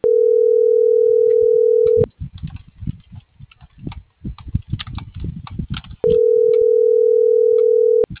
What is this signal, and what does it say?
Ringback tone (North America), 440 + 480 Hz.